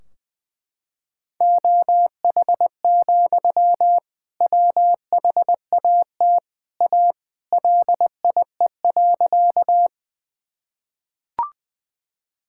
Morse code, heard as a keyed tone, about 20 wpm, 700 Hz. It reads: OH, WHAT A LIE.
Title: Sentences from Top 1600 Words 20wpm